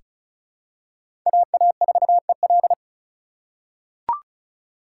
Morse code, heard as a keyed tone, about 34 wpm, 700 Hz. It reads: AA4EL
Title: USA Call Signs - 5 Characters - 2x Character Spacing - 35wpm